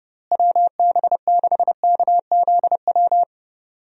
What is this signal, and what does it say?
Morse code, heard as a keyed tone, about 30 wpm, 700 Hz. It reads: WB6KZW
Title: USA Call Signs - 6 Characters - Repeated 3x - 30wpm